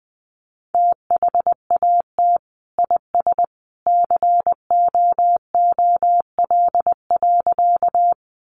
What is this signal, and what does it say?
Morse code, heard as a keyed tone, about 20 wpm, 700 Hz. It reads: THAT IS COOL.
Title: Sentences from Top 1200 Words 20wpm